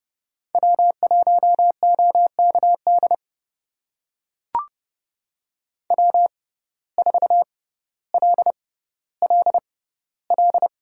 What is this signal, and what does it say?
Morse code, heard as a keyed tone, about 30 wpm, spaced as for 15 wpm, 700 Hz. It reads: W1OKD W4LLL
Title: USA Call Signs - 5 Characters - 6x Character Spacing - 30wpm